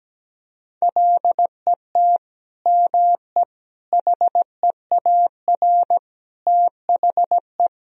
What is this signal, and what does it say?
Morse code, heard as a keyed tone, about 17 wpm, 700 Hz. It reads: LET ME HEAR THE
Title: Sentences from Top 1000 Words - Review Part 1 - 17wpm